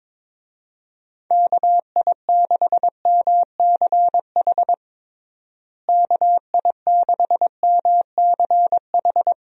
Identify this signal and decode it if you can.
Morse code, heard as a keyed tone, about 22 wpm, 700 Hz. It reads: KI6MCH KI6MCH
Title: USA Call Signs - 6 Characters - Repeated 2x - 22wpm